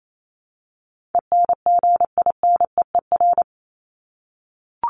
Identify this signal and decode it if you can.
Morse code, heard as a keyed tone, about 28 wpm, 700 Hz. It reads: ENGINEER
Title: ING Suffix - Top 100 Words with Base Words 28wpm